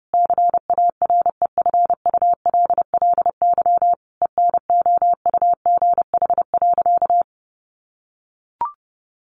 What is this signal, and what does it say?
Morse code, heard as a keyed tone, about 30 wpm, 700 Hz. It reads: CAREFULLY ENOUGH.
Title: Sentences from Top 1300 Words 30wpm